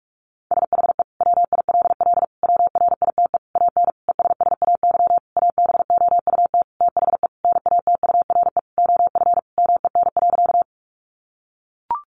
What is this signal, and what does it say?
Morse code, heard as a keyed tone, about 44 wpm, 700 Hz. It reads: SHE WILL WRITE AN ESSAY ABOUT THE NATURE OF MEN.
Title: Sentences from Top 1500 Words 45wpm